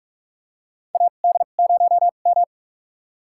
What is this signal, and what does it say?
Morse code, heard as a keyed tone, about 46 wpm, spaced as for 32 wpm, 700 Hz. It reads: AD0M
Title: USA Call Signs - 2x Character Spacing - 45wpm